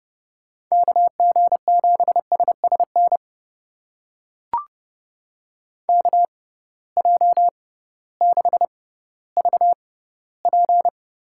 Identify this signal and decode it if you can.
Morse code, heard as a keyed tone, about 30 wpm, spaced as for 16 wpm, 700 Hz. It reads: KG7SSN KJ6VP
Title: USA Call Signs - 6 Characters - 6x Character Spacing - 30wpm